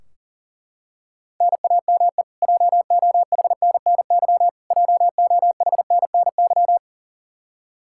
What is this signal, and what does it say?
Morse code, heard as a keyed tone, about 38 wpm, 700 Hz. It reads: NAME JOHNNY JOHNNY
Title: Practice QSO - Exchange 1 - Beginning - 40wpm